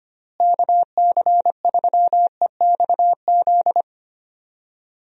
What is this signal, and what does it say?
Morse code, heard as a keyed tone, about 25 wpm, 700 Hz. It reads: KC3EXZ
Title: USA Call Signs - 6 Characters - Repeated 3x - 25wpm